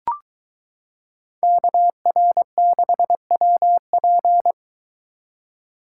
Morse code, heard as a keyed tone, about 23 wpm, 700 Hz. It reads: KR6WP